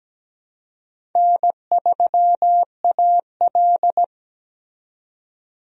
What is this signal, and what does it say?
Morse code, heard as a keyed tone, about 17 wpm, 700 Hz. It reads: N3AL
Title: USA Call Signs - 4 Characters - 17wpm